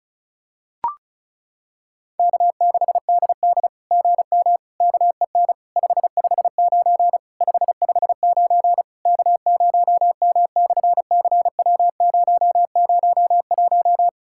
Morse code, heard as a keyed tone, about 36 wpm, 700 Hz. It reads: K6DD GM KEN 559 559 K0M/CW001
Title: SOTA QSO Practice 35wpm